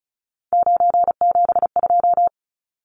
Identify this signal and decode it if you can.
Morse code, heard as a keyed tone, about 34 wpm, 700 Hz. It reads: 972
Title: Numbers - 3-Digit 35wpm